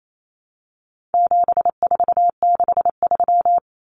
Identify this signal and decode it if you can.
Morse code, heard as a keyed tone, about 28 wpm, 700 Hz. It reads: 7463
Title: Common Cut Numbers - 4-Digit 28wpm